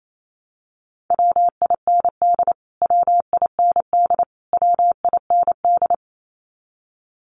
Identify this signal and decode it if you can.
Morse code, heard as a keyed tone, about 28 wpm, 700 Hz. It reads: WIND WIND WIND
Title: Top 500 Words Repeated 28wpm